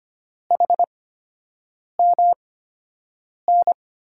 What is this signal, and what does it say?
Morse code, heard as a keyed tone, about 25 wpm, spaced as for 7 wpm, 700 Hz. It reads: HMN
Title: USA Call Signs - 5 Characters - 8x Character Spacing - 25wpm